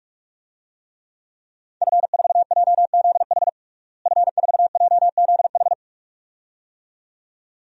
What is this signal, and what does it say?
Morse code, heard as a keyed tone, about 45 wpm, 700 Hz. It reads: F4JZH F4JZH
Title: DXCC-100 Call Signs - 5 Characters - Repeated 2x - 45wpm